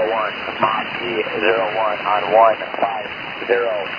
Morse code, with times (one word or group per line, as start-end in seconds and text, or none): none